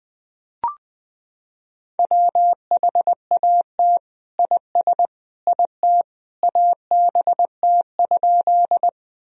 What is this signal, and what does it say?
Morse code, heard as a keyed tone, about 20 wpm, 700 Hz. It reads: WHAT IS IT ABT?
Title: Sentences from Top 100 Words and Abbreviations 20wpm